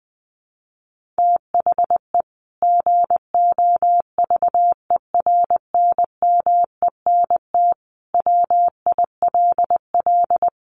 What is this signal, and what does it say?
Morse code, heard as a keyed tone, about 20 wpm, 700 Hz. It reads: THE GOVERNMENT WILL
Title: Sentences from Top 1500 Words 20wpm